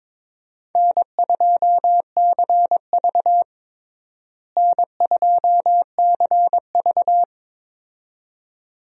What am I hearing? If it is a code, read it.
Morse code, heard as a keyed tone, about 22 wpm, 700 Hz. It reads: N2CV N2CV
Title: USA Call Signs - 4 Characters - Repeated 2x - 22wpm